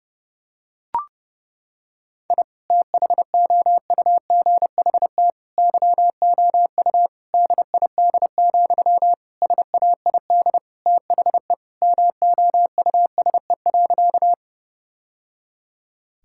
Morse code, heard as a keyed tone, about 32 wpm, 700 Hz. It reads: I THOUGHT YOU DID, SAID THE MOUSE.
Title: Sentences from Top 1000 Words - Review Part 5 - 30wpm